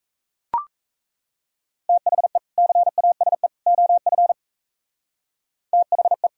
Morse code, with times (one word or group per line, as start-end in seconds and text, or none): 1.9-2.4 THE
2.6-3.5 CASE
3.7-4.3 OF
5.7-6.3 THE